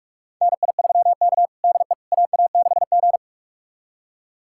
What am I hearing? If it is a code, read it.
Morse code, heard as a keyed tone, about 45 wpm, 700 Hz. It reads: NI3K DE AA6G
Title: USA Callsign DE Callsign - 4 Characters - 45wpm